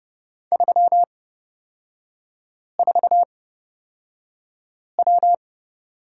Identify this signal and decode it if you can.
Morse code, heard as a keyed tone, about 30 wpm, spaced as for 5 wpm, 700 Hz. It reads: 34W
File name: Lesson 38e - Letter q - Three Characters 30wpm